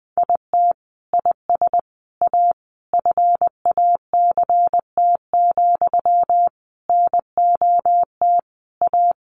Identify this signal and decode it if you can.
Morse code, heard as a keyed tone, about 20 wpm, 700 Hz. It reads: IT IS A FACT, NOT A
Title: Sentences from Top 1400 Words 20wpm